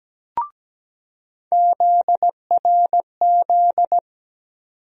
Morse code, heard as a keyed tone, about 17 wpm, 700 Hz. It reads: ZRZ
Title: USA Call Sign Postfixes - 3 Letters 17wpm